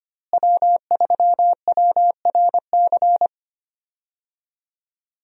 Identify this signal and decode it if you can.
Morse code, heard as a keyed tone, about 25 wpm, 700 Hz. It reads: W3WRC
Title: USA Call Signs - 5 Characters - Repeated 2x - 25wpm